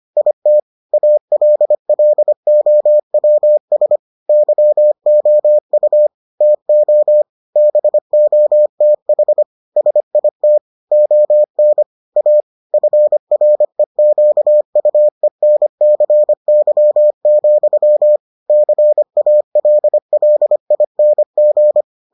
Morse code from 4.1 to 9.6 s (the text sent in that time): YOU TO BOTH